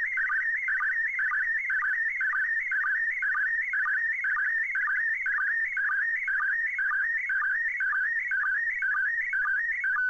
SSTV (slow-scan television), PD 120.